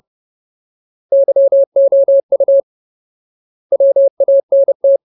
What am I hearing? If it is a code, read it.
Morse code, heard as a keyed tone, about 30 wpm, 550 Hz. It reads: YOU WANT
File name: Ultimate Word List / 30WPM Ultimate